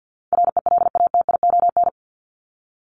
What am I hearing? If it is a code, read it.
Morse code, heard as a keyed tone, about 50 wpm, 700 Hz. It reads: RELATION
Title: ES:S Suffix - Top 100 Words with Base Words 50wpm